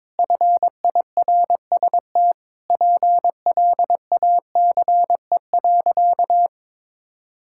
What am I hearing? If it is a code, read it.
Morse code, heard as a keyed tone, about 22 wpm, 700 Hz. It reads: FIRST PLACE.
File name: Sentences with 7-Word Phrases - 22wpm